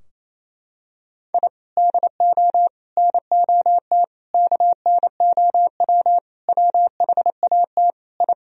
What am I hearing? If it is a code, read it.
Morse code, heard as a keyed tone, about 27 wpm, 700 Hz. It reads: I DO NOT KNOW WHAT I
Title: Sentences from Top 1000 Words - Review Part 3 - 28wpm